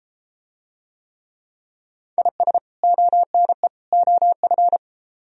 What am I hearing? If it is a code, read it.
Morse code, heard as a keyed tone, about 33 wpm, 700 Hz. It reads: IS ONE OF